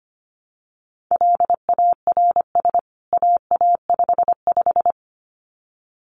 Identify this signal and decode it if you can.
Morse code, heard as a keyed tone, about 25 wpm, 700 Hz. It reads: LARS AA55
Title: CWT Exchange 25wpm